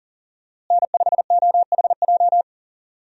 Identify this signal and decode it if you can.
Morse code, heard as a keyed tone, about 40 wpm, 700 Hz. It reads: N5OHJ
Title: USA Call Signs - 5 Characters - Repeated 3x - 40wpm